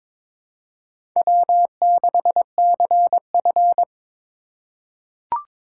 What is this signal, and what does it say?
Morse code, heard as a keyed tone, about 22 wpm, 700 Hz. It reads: W6CF
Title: USA Call Signs - 4 Characters - 6x Character Spacing - 22wpm